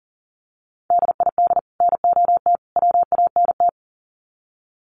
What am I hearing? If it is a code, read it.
Morse code, heard as a keyed tone, about 40 wpm, 700 Hz. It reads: DID NOT WANT
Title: Sets of 3 Words 40wpm